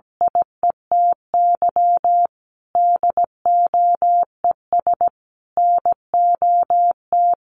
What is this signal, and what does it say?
Morse code, heard as a keyed tone, about 17 wpm, 700 Hz. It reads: IETY DOES NOT
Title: Sentences from Top 1400 Words 17wpm